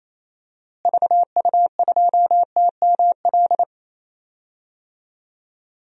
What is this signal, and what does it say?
Morse code, heard as a keyed tone, about 27 wpm, 700 Hz. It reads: VU2TML
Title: DXCC-100 Call Signs - 6 Characters - Repeated 2x - 28wpm